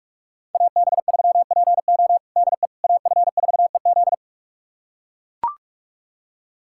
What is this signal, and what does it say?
Morse code, heard as a keyed tone, about 41 wpm, 700 Hz. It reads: AB3PO DE AF4EZ
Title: USA Callsign DE Callsign - 5 Characters - 45wpm